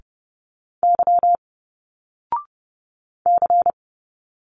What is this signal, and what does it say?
Morse code, heard as a keyed tone, about 30 wpm, 700 Hz. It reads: Y C
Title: Single Letters 30wpm